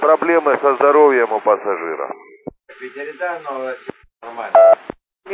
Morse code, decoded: none

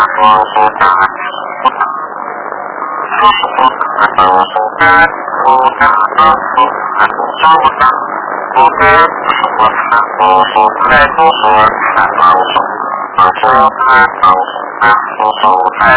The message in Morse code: none

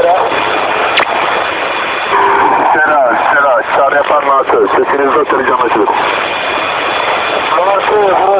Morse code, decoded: none

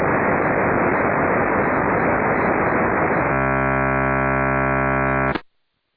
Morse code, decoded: none

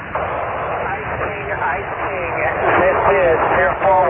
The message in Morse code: none

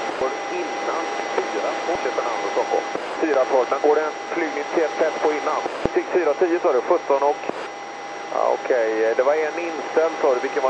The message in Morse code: none

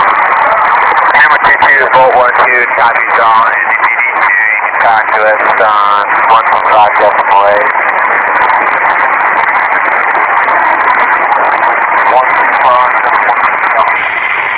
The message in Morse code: none

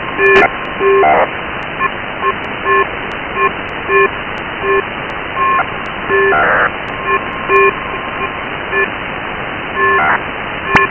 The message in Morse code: none

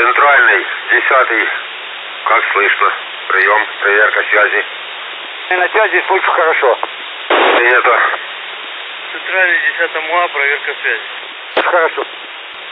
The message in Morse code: none

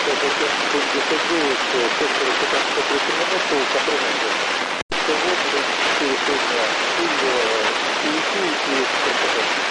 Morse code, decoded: none

